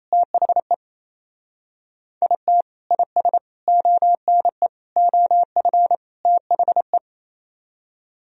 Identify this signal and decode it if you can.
Morse code, heard as a keyed tone, about 34 wpm, 700 Hz. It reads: THE IT IS ONE OF THE
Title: Speed-Racing - Sets of 5 Words 28wpm